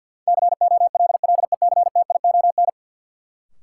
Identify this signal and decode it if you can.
Morse code, heard as a keyed tone, about 53 wpm, 700 Hz. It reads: COLLECTION
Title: Sentences from Top 1200 Words - New Word Drill Alt - 50wpm